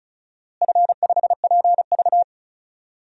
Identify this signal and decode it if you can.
Morse code, heard as a keyed tone, about 36 wpm, 700 Hz. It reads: F5PV
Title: DXCC-100 Call Signs - 4 Characters - Repeated 2x - 35wpm